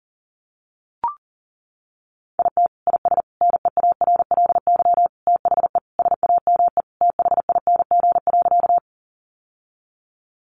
Morse code, heard as a keyed tone, about 40 wpm, 700 Hz. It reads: IT IS NEARLY THE SAME THING.
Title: Sentences from Top 1300 Words 40wpm